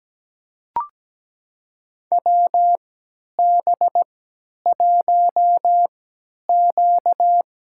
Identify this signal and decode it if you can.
Morse code, heard as a keyed tone, about 17 wpm, spaced as for 10 wpm, 700 Hz. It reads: WB1Q
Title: USA Call Signs - 4 Characters - 3x Character Spacing - 17wpm